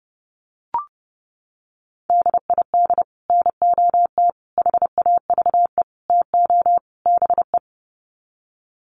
Morse code, heard as a keyed tone, about 30 wpm, 700 Hz. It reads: DID NOT HAVE TO BE
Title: Sets of 5 Words Encore2 30wpm